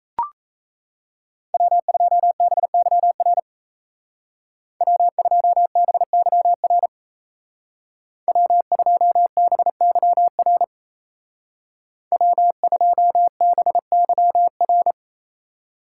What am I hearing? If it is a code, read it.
Morse code, heard as a keyed tone, about 35 wpm, 700 Hz. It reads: W2BYR W2BYR W2BYR W2BYR
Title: Speed Racing - USA 1x3 Call Signs 28wpm